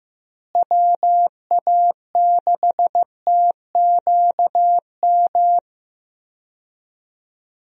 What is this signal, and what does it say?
Morse code, heard as a keyed tone, about 15 wpm, 700 Hz. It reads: WA6TQM
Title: USA Call Signs - 6 Characters - Repeated 2x - 15wpm